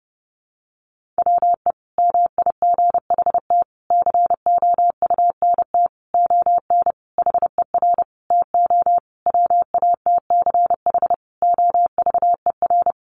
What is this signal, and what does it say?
Morse code, heard as a keyed tone, about 30 wpm, 700 Hz. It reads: WE MIGHT COUNT ON HER TO WATCH OVER
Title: Sentences from Top 1500 Words 30wpm